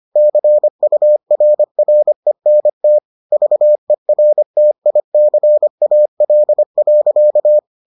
Morse code, heard as a keyed tone, about 25 wpm, 600 Hz. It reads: CURRENT VERTICAL.